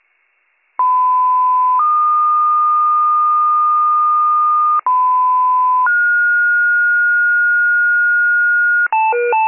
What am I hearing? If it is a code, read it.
Two-tone page, heard as a one-second tone, then a three-second tone, about 1010 then 1220 Hz, about 1010 then 1430 Hz, with noise. It is heard from 0.8 to 8.9 s.